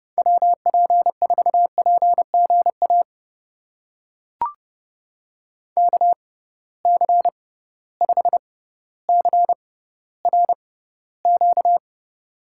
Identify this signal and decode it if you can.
Morse code, heard as a keyed tone, about 31 wpm, spaced as for 15 wpm, 700 Hz. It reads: WP4PGA KC5CRQ
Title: USA Call Signs - 6 Characters - 6x Character Spacing - 30wpm